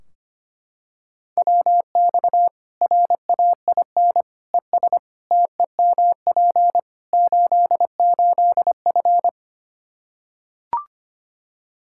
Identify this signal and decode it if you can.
Morse code, heard as a keyed tone, about 25 wpm, 700 Hz. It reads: WX RAIN ES TEMP 88F
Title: Practice QSO - Exchange 2 - Info - 25wpm